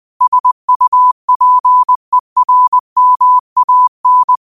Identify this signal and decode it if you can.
Morse code, heard as a keyed tone, about 20 wpm, 1000 Hz. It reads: SUPERMAN
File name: qotdfiles / 20wpm